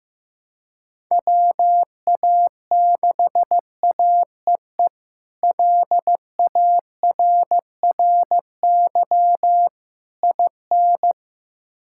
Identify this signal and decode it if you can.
Morse code, heard as a keyed tone, about 16 wpm, 700 Hz. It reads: WA6AEE LARRY IN
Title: K1USN SST QSO Practice - 15wpm